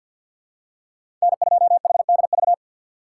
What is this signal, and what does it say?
Morse code, heard as a keyed tone, about 48 wpm, 700 Hz. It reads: N2HDV
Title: USA Call Signs - 5 Characters - 3x Character Spacing - 50wpm